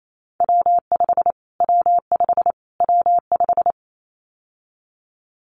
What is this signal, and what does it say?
Morse code, heard as a keyed tone, about 28 wpm, 700 Hz. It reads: W5 W5 W5
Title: Callsign Prefixes for Top 25 DXCC - Repeated 3x - 28wpm